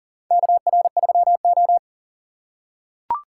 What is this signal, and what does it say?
Morse code, heard as a keyed tone, about 40 wpm, 700 Hz. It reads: KR3O
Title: USA Call Signs - 4 Characters - 3x Character Spacing - 40wpm